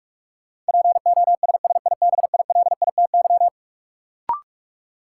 Morse code, heard as a keyed tone, about 46 wpm, 700 Hz. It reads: POSSIBILITY
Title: Sets of 3 Words - New Word Drill - 45wpm